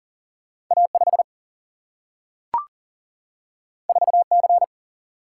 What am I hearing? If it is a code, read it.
Morse code, heard as a keyed tone, about 40 wpm, 700 Hz. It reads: A 5 4 C
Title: Single-Letter Single-Number 40wpm